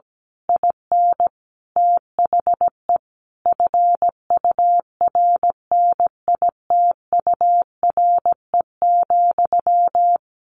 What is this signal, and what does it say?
Morse code, heard as a keyed tone, about 17 wpm, 700 Hz. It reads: IN THE FURNITURE,
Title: A Sound Of Thunder 17wpm